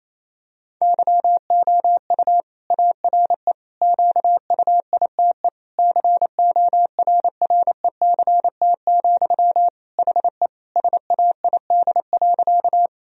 Morse code, heard as a keyed tone, about 28 wpm, 700 Hz. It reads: YOU ARE QUITE CORRECT, HE SAID.
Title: Sentences from Top 1000 Words - Review Part 2 - 28wpm